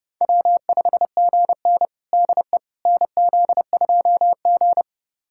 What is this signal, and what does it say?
Morse code, heard as a keyed tone, about 30 wpm, 700 Hz. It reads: W5GN DE NZ2G
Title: USA Callsign DE Callsign - 4 Characters - 30wpm